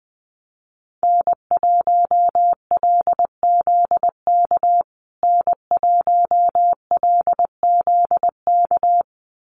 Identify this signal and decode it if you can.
Morse code, heard as a keyed tone, about 20 wpm, 700 Hz. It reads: N1LZK N1LZK
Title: USA Call Signs Repeated 2x 20wpm